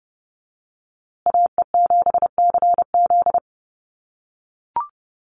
Morse code, heard as a keyed tone, about 30 wpm, 700 Hz. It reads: AE7CZ
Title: USA Call Signs Repeated 3x 30wpm